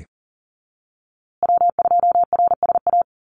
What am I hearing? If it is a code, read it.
Morse code, heard as a keyed tone, about 40 wpm, 700 Hz. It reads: W2RSA
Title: USA Call Signs Repeated 3x 40wpm